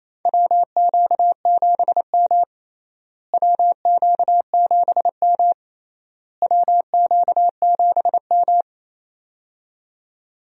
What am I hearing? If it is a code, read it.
Morse code, heard as a keyed tone, about 28 wpm, 700 Hz. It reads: WQ7M WQ7M WQ7M
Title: USA Call Signs - 4 Characters - Repeated 3x - 28wpm